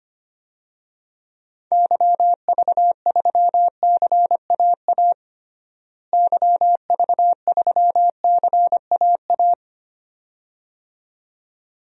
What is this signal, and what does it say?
Morse code, heard as a keyed tone, about 25 wpm, 700 Hz. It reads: YV3CAA YV3CAA
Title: DXCC-100 Call Signs - 6 Characters - Repeated 2x - 25wpm